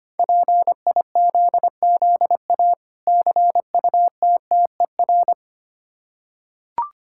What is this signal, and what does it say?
Morse code, heard as a keyed tone, about 25 wpm, 700 Hz. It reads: PIZZA CUTTER
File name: practice / 25wpm / Difficult Words with Hints - 25wpm